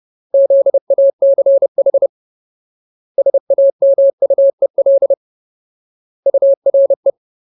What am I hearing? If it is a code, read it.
Morse code, heard as a keyed tone, about 30 wpm, 550 Hz. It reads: ZACH SAMUEL URE
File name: Ultimate Word List / 30WPM Ultimate